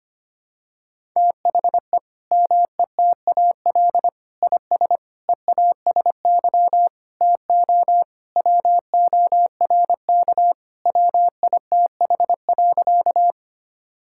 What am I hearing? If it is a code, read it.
Morse code, heard as a keyed tone, about 25 wpm, 700 Hz. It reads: THE METAL IS EASY TO WORK WITH.
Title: Sentences from Top 1000 Words - Review Part 6 - 25wpm